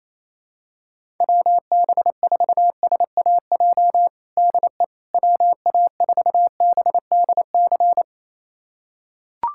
Morse code, heard as a keyed tone, about 28 wpm, 700 Hz. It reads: WB4SAJ DE WA4BDC
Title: USA Callsign DE Callsign - 6 Characters - 28wpm